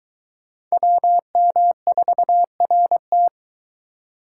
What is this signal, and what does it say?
Morse code, heard as a keyed tone, about 22 wpm, 700 Hz. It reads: WM4RT